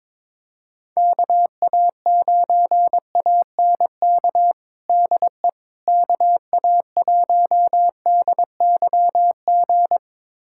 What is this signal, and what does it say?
Morse code, heard as a keyed tone, about 22 wpm, 700 Hz. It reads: KA9ANK DE KA1DYG
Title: USA Callsign DE Callsign - 6 Characters - 22wpm